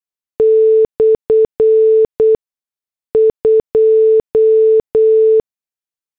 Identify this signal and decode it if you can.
Morse code, heard as a keyed tone, about 8 wpm, spaced as for 6 wpm, 440 Hz. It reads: /2